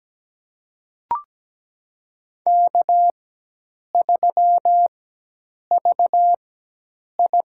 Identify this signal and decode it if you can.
Morse code, heard as a keyed tone, about 17 wpm, spaced as for 8 wpm, 700 Hz. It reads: K3VI
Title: USA Call Signs - 4 Characters - 4x Character Spacing - 17wpm